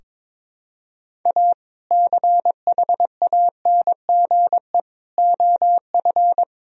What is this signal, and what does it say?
Morse code, heard as a keyed tone, about 21 wpm, 700 Hz. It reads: A CHANGE OF
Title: Sentences from Top 1000 Words - Review Part 2 - 22wpm